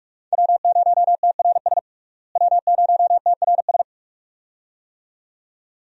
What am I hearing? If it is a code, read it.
Morse code, heard as a keyed tone, about 42 wpm, 700 Hz. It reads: W0TRS W0TRS
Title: USA Call Signs - 5 Characters - Repeated 2x - 45wpm